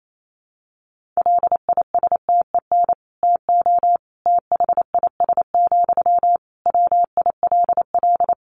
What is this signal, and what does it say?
Morse code, heard as a keyed tone, about 28 wpm, 700 Hz. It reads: LISTEN TO THIS, WILL
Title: Sentences from Top 400 Words 28wpm